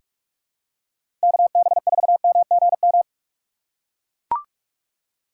Morse code, heard as a keyed tone, about 41 wpm, 700 Hz. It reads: KB4MGM
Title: USA Call Signs - 2x Character Spacing - 45wpm